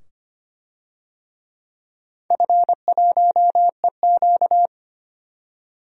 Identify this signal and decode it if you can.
Morse code, heard as a keyed tone, about 25 wpm, 700 Hz. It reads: F1EQ
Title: DXCC-100 Call Signs - 4 Characters - 25wpm